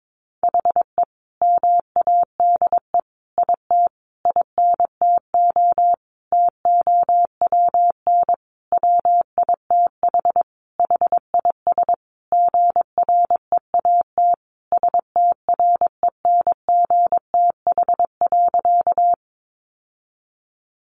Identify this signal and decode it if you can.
Morse code, heard as a keyed tone, about 22 wpm, 700 Hz. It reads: HE MADE IT INTO TOWN WITH HIS GREAT STRENGTH.
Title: Sentences from Top 1300 Words 22wpm